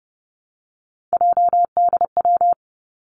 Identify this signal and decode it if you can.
Morse code, heard as a keyed tone, about 30 wpm, 700 Hz. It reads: JDW
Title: Three Challenging Letter-Number Combinations 30wpm